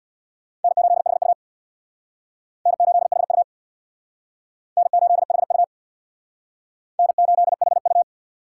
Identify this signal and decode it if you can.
Morse code, heard as a keyed tone, about 63 wpm, 700 Hz. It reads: N8HU N8HU N8HU N8HU